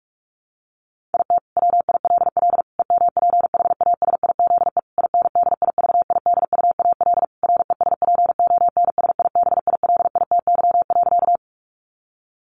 Morse code, heard as a keyed tone, about 44 wpm, 700 Hz. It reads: IT WILL EMPHASIZE INDIVIDUAL RESPONSIBILITY.
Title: Sentences from Top 1500 Words 45wpm